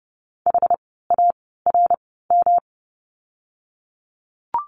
Morse code, heard as a keyed tone, about 30 wpm, spaced as for 17 wpm, 700 Hz. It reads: HARM
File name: course / 30:10wpm / 108 Lesson 18f - Letter m - Words 30wpm